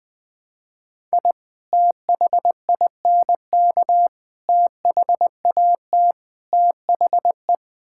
Morse code, heard as a keyed tone, about 20 wpm, 700 Hz. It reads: I THINK THAT THE
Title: Build A Phrase - 4-Word Phrases - 20wpm